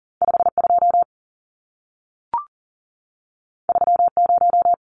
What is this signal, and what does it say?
Morse code, heard as a keyed tone, about 40 wpm, 700 Hz. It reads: 52 30
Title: Numbers - 2-Digit 40wpm